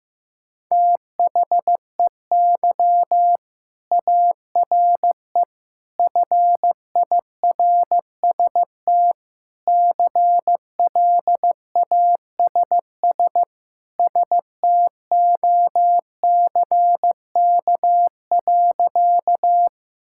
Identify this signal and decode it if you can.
Morse code, heard as a keyed tone, about 15 wpm, 700 Hz. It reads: THEY ARE FIRST CLASS STOCK.
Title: Sentences from Top 1000 Words - Review Part 5 - 15wpm